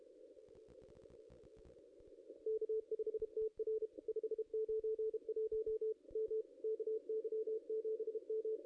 Morse code, heard as a keyed tone, about 35 wpm, 440 Hz. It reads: K5TR 591 M KY7M
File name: contests / ss-cw-k5tr-op-n5rz -0400